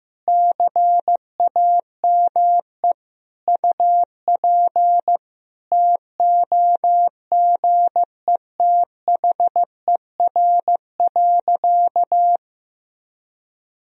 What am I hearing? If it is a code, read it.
Morse code, heard as a keyed tone, about 15 wpm, 700 Hz. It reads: CAME UP TOGETHER.
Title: Sentences from Top 1000 Words - Review Part 1 - 15wpm